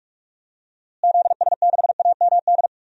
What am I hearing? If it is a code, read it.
Morse code, heard as a keyed tone, about 46 wpm, 700 Hz. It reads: ZS6AMD
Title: DXCC-100 Call Signs - 6 Characters - Repeated 2x - 45wpm